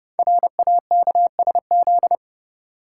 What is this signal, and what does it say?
Morse code, heard as a keyed tone, about 30 wpm, 700 Hz. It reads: RAKSZ
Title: Code Groups - 5 Letter:Numbers - 30wpm